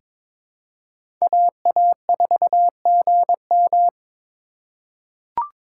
Morse code, heard as a keyed tone, about 22 wpm, 700 Hz. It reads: AA4GM
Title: USA Call Signs - 5 Characters - 4x Character Spacing - 22wpm